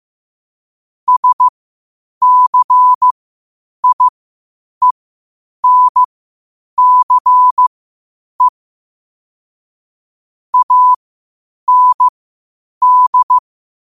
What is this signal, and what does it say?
Morse code, heard as a keyed tone, about 15 wpm, spaced as for 8 wpm, 1000 Hz. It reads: SCIENCE AND
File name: qotdfiles / 8wpm